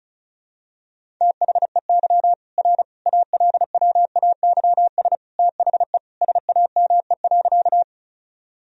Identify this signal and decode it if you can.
Morse code, heard as a keyed tone, about 34 wpm, 700 Hz. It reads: THEY R ALWAYS THE SAME.
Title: Sentences from Top 300 Words and Abbreviations 35wpm